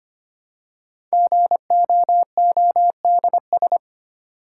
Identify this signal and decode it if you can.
Morse code, heard as a keyed tone, about 25 wpm, 700 Hz. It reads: GOODS
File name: 5 Letter Words - First 3 Letters Given - 25wpm